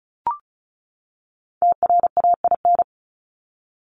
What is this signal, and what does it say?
Morse code, heard as a keyed tone, about 34 wpm, 700 Hz. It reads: TRAIN
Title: ING Suffix - Top 100 Words with Base Words 35wpm